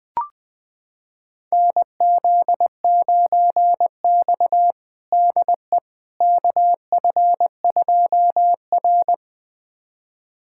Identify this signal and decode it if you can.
Morse code, heard as a keyed tone, about 20 wpm, 700 Hz. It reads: NZ9X DE KF2R
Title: USA Callsign DE Callsign - 4 Characters - 20wpm